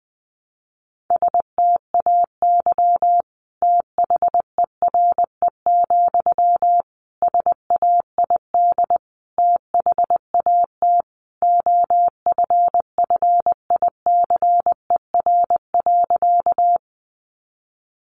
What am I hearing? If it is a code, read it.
Morse code, heard as a keyed tone, about 20 wpm, 700 Hz. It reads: STAY THERE, SAID THAT OFFICER.
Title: Sentences from Top 1200 Words 20wpm